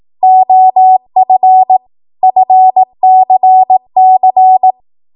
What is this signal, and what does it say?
Morse code, heard as a keyed tone, about 18 wpm, 750 Hz. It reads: OF FCC
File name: Morse / 18WPM